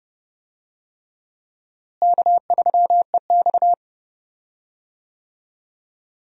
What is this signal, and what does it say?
Morse code, heard as a keyed tone, about 30 wpm, 700 Hz. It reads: K3EX
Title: USA Call Signs - 4 Characters - Repeated 3x - 30wpm